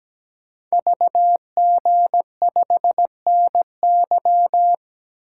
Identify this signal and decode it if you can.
Morse code, heard as a keyed tone, about 17 wpm, 700 Hz. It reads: VG5NY
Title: Code Groups - 5 Letter:Numbers - 17wpm